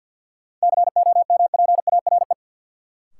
Morse code, heard as a keyed tone, about 52 wpm, 700 Hz. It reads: COMPARE
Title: Sentences from Top 1300 Words - New Word Drill Alt - 50wpm